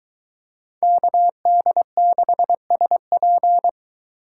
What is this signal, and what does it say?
Morse code, heard as a keyed tone, about 23 wpm, 700 Hz. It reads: KD6SP